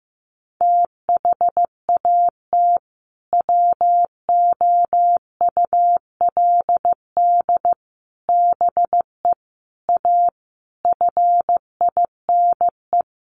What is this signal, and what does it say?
Morse code, heard as a keyed tone, about 15 wpm, 700 Hz. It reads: THAT WOULD BE A FINE
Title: Sentences from Top 500 Words 15wpm